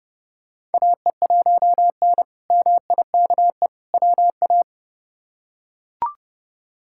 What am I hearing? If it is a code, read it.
Morse code, heard as a keyed tone, about 29 wpm, 700 Hz. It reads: AE1N MIKE WA
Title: K1USN SST QSO Practice - 30wpm